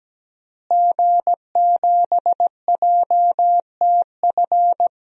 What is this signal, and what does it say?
Morse code, heard as a keyed tone, about 17 wpm, 700 Hz. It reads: G7JTF
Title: DXCC-100 Call Signs - 5 Characters - Repeated 3x - 17wpm